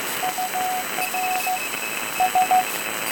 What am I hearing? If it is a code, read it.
Morse code, heard as a keyed tone, about 14 wpm, 740 Hz, with noise. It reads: UR S